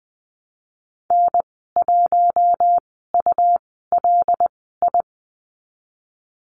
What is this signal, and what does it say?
Morse code, heard as a keyed tone, about 20 wpm, spaced as for 15 wpm, 700 Hz. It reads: N1ULI
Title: 147 Lesson 24g - Letter y - Call Signs 20wpm